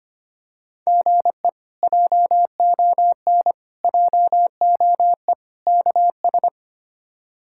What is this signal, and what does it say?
Morse code, heard as a keyed tone, about 25 wpm, 700 Hz. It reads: GE JON JOE KS
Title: K1USN SST QSO Practice - 25wpm